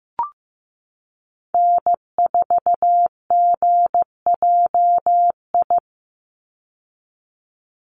Morse code, heard as a keyed tone, about 15 wpm, 700 Hz. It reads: N4GJI
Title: USA Call Signs 15wpm